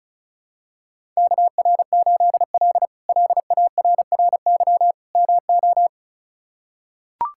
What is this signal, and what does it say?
Morse code, heard as a keyed tone, about 36 wpm, 700 Hz. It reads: KR8L LARRY MO
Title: K1USN SST QSO Practice - 35wpm